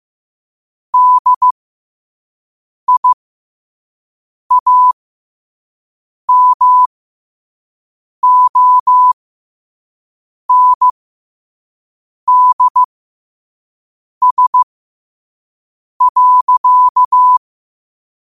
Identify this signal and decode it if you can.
Morse code, heard as a keyed tone, about 15 wpm, spaced as for 5 wpm, 1000 Hz. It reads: DIAMONDS.
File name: qotdfiles / 5wpm